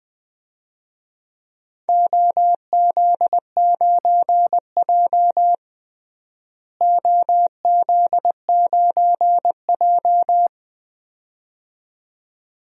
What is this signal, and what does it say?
Morse code, heard as a keyed tone, about 20 wpm, 700 Hz. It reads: OZ9J OZ9J
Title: DXCC-100 Call Signs - 4 Characters - Repeated 2x - 20wpm